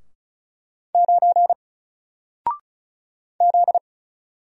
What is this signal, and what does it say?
Morse code, heard as a keyed tone, about 33 wpm, 700 Hz. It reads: 9 Z
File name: Single-Letter-Number-35wpm